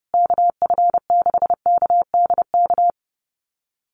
Morse code, heard as a keyed tone, about 28 wpm, 700 Hz. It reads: KF6KDK
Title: USA Call Signs Repeated 3x 30wpm